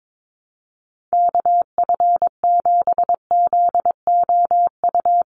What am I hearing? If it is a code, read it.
Morse code, heard as a keyed tone, about 22 wpm, 700 Hz. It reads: KF7ZOU
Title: USA Call Signs Repeated 3x 22wpm